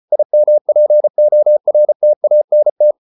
Morse code, heard as a keyed tone, about 34 wpm, 600 Hz. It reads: IMPORTANT